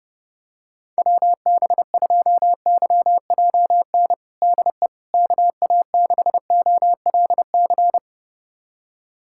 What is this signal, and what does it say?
Morse code, heard as a keyed tone, about 30 wpm, 700 Hz. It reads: WB2YJN DE KA6OLC
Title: USA Callsign DE Callsign - 6 Characters - 30wpm